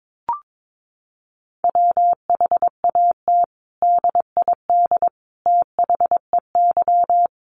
Morse code, heard as a keyed tone, about 22 wpm, 700 Hz. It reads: WHAT DID THEY
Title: Sentences from Top 100 Words 22wpm